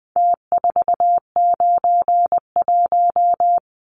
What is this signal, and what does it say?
Morse code, heard as a keyed tone, about 20 wpm, 700 Hz. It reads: T491
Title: Numbers - Contest Style - 4-Digit 20wpm